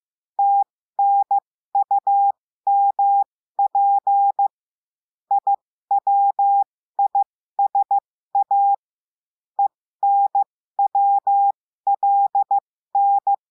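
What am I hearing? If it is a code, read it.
Morse code, heard as a keyed tone, about 15 wpm, spaced as for 13 wpm, 800 Hz. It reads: TNUMP IWISA ENWLN
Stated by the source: Morse code lesson adding the letter I, with five-letter code groups.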